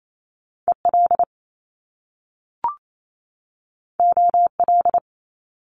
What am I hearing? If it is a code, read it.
Morse code, heard as a keyed tone, about 28 wpm, 700 Hz. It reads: EL OL
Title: Top Two Letter Combinations in Words 28wpm